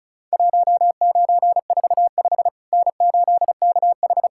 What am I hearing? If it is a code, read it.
Morse code, heard as a keyed tone, about 36 wpm, 700 Hz. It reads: 1945 N8KH
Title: Worked All Europe DX Contest - QTC Exchange - 35wpm